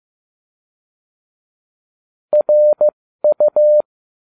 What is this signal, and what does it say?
Morse code, heard as a keyed tone, about 15 wpm, spaced as for 13 wpm, 600 Hz. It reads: RU